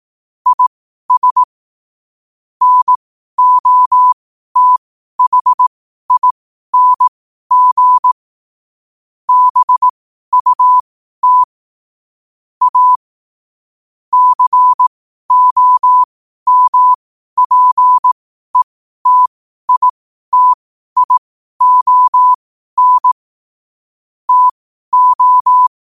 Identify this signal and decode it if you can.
Morse code, heard as a keyed tone, about 18 wpm, spaced as for 12 wpm, 1000 Hz. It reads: IS NOTHING BUT A COMPETITION TO
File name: qotdfiles / 12wpm